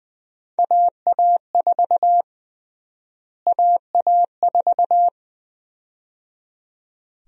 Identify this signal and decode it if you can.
Morse code, heard as a keyed tone, about 20 wpm, 700 Hz. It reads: AA4 AA4
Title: USA Call Sign Prefixes - Repeated 2x - 20wpm